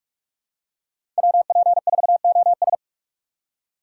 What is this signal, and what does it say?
Morse code, heard as a keyed tone, about 45 wpm, 700 Hz. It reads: WP4OS
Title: USA Call Signs - 5 Characters - 6x Character Spacing - 45wpm